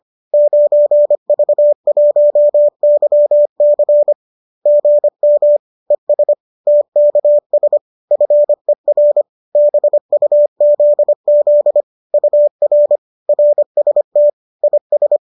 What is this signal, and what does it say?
Morse code, heard as a keyed tone, about 25 wpm, 590 Hz. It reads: 9V1YC GM ES TKS FER BUZZ UR RST IS